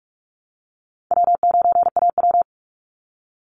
Morse code, heard as a keyed tone, about 44 wpm, 700 Hz. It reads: W9AW
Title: Easy Call Signs Repeated 3x - 45wpm